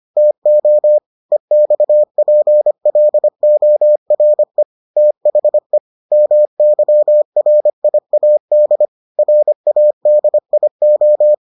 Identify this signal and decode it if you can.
Morse code, heard as a keyed tone, about 25 wpm, 600 Hz. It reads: TO EXPLORE THE MYRIAD RADIO